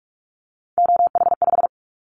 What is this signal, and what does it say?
Morse code, heard as a keyed tone, about 46 wpm, 700 Hz. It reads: OH5
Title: Callsign Prefixes for Top 50 DXCC - 45wpm